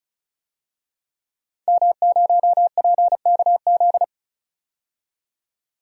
Morse code, heard as a keyed tone, about 36 wpm, 700 Hz. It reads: M0PKZ
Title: DXCC-100 Call Signs - 5 Characters - 35wpm